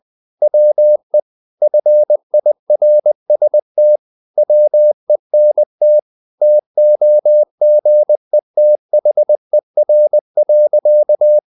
Morse code, heard as a keyed tone, about 20 wpm, 600 Hz. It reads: WE FIRST WENT TOGETHER.
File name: the old man and the sea / the old man and the sea part 1 20wpm